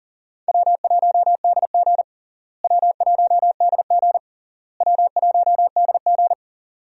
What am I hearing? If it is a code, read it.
Morse code, heard as a keyed tone, about 40 wpm, 700 Hz. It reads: W1DG W1DG W1DG
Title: USA Call Signs - 4 Characters - Repeated 3x - 40wpm